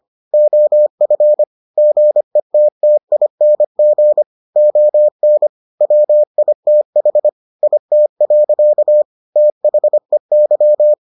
Morse code, heard as a keyed tone, about 25 wpm, 600 Hz. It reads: OF GETTING ON WITH IT. THEY